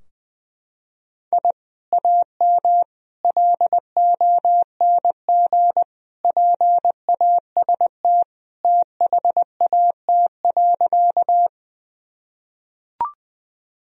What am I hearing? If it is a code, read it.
Morse code, heard as a keyed tone, about 19 wpm, 700 Hz. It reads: I AM LONG PAST THAT.
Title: Sentences from Top 1000 Words - Review Part 3 - 20wpm